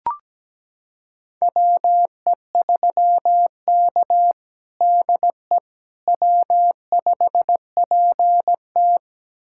Morse code, heard as a keyed tone, about 17 wpm, 700 Hz. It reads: WE3K DE W5PT
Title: USA Callsign DE Callsign - 4 Characters - 17wpm